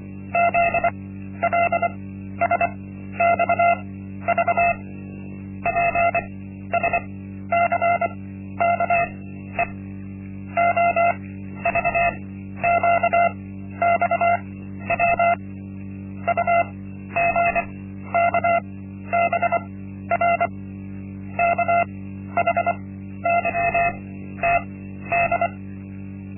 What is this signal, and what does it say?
Morse code, heard as a keyed tone, about 25 wpm, spaced as for 12 wpm, 690 Hz. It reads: ZLSXV PSCKE OVQXW UGKBR KHYTD
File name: Linksmakalnis-1